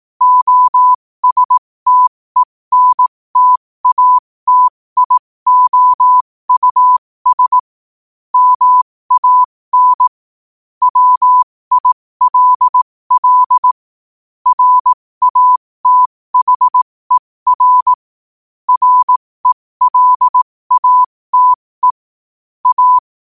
Morse code, heard as a keyed tone, about 18 wpm, spaced as for 15 wpm, 1000 Hz. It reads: OSTENTATIOUS MAN WILL RATHER RELATE A
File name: qotdfiles / 15wpm